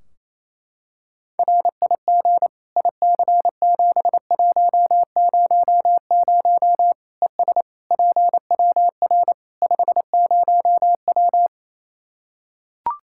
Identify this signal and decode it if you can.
Morse code, heard as a keyed tone, about 28 wpm, 700 Hz. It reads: RIG IC7100 ES PWR 50W
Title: Practice QSO - Exchange 2 - Info - 28wpm